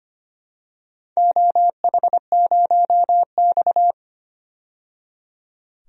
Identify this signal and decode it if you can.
Morse code, heard as a keyed tone, about 25 wpm, 700 Hz. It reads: OH0X
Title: DXCC-100 Call Signs - 4 Characters - 25wpm